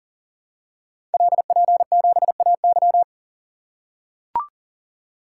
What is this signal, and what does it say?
Morse code, heard as a keyed tone, about 38 wpm, 690 Hz. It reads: LP7AY
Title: Code Groups - 5 Letter:Numbers - 40wpm